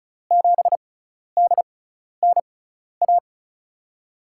Morse code, heard as a keyed tone, about 36 wpm, spaced as for 12 wpm, 700 Hz. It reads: 7DNA
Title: USA Call Signs - 5 Characters - 6x Character Spacing - 35wpm